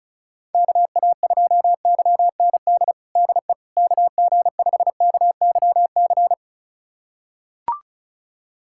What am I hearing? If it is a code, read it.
Morse code, heard as a keyed tone, about 36 wpm, 700 Hz. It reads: KA2YND DE KG5KYC
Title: USA Callsign DE Callsign - 6 Characters - 35wpm